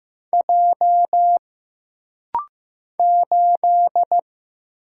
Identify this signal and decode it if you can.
Morse code, heard as a keyed tone, about 15 wpm, 700 Hz. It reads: J 8
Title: Single Letter-Number Alt - Rapid-Fire 15wpm